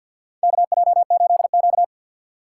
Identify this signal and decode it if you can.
Morse code, heard as a keyed tone, about 52 wpm, 700 Hz. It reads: KJ8Q
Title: USA Call Signs - 4 Characters - 3x Character Spacing - 50wpm